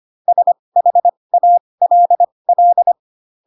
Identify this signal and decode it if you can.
Morse code, heard as a keyed tone, about 25 wpm, spaced as for 20 wpm, 700 Hz. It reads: SHALL
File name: Pinocchio / Pinocchio ch7 15 25 ik1hge